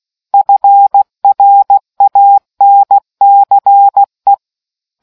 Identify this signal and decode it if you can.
Morse code, heard as a keyed tone, about 16 wpm, 800 Hz. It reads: FRANCE